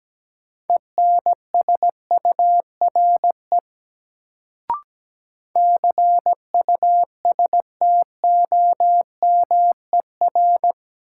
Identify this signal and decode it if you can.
Morse code, heard as a keyed tone, about 17 wpm, 700 Hz. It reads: ENSURE CUSTOMER
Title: Sentences from Top 1100 Words - New Word Drill Alt - 17wpm